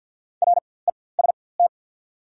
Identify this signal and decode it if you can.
Morse code, heard as a keyed tone, about 49 wpm, spaced as for 23 wpm, 690 Hz. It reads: REST
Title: 4x Character Spacing - Top 500 Words - 50wpm